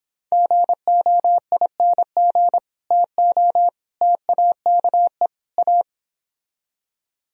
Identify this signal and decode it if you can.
Morse code, heard as a keyed tone, about 26 wpm, 700 Hz. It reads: GOING TO TAKE A